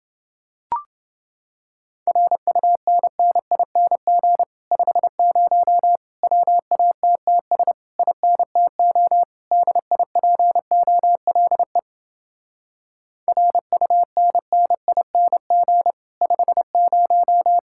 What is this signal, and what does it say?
Morse code, heard as a keyed tone, about 28 wpm, 700 Hz. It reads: RUNNING 50 WATTS INTO DIPOLE RUNNING 50